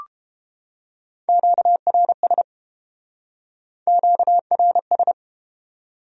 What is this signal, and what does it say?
Morse code, heard as a keyed tone, about 31 wpm, 700 Hz. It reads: QRS QRS